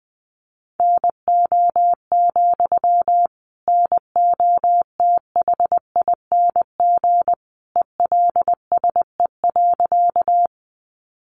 Morse code, heard as a keyed tone, about 20 wpm, 700 Hz. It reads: NO, NOTHING ELSE.
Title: Sentences from Top 800 Words 20wpm